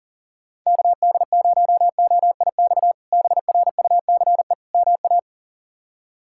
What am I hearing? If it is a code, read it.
Morse code, heard as a keyed tone, about 40 wpm, 700 Hz. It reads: KD0OIX BRUCE MA
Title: K1USN SST QSO Practice - 40wpm